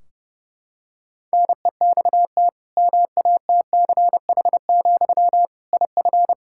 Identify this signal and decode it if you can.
Morse code, heard as a keyed tone, about 30 wpm, 700 Hz. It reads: NEXT MATCH, IF
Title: Sentences from Top 1600 Words 30wpm